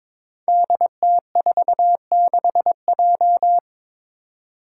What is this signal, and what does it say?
Morse code, heard as a keyed tone, about 21 wpm, 700 Hz. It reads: DT46J
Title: Five Challenging Letter-Number Combinations 22wpm